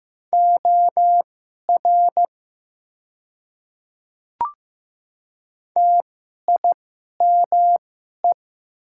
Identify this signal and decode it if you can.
Morse code, heard as a keyed tone, about 15 wpm, spaced as for 11 wpm, 700 Hz. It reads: OR TIME
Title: 2x Character Spacing - Top 100 Words - 15wpm